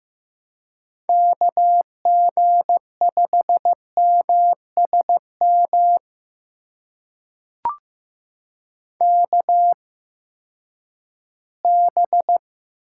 Morse code, heard as a keyed tone, about 15 wpm, spaced as for 10 wpm, 700 Hz. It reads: KG5MSM KB
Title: USA Call Signs - 6 Characters - 8x Character Spacing - 15wpm